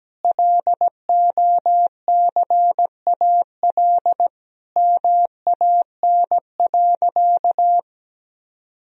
Morse code, heard as a keyed tone, about 17 wpm, 700 Hz. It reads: LOCAL MAN.
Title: Sentences from Top 1000 Words - Review Part 3 - 17wpm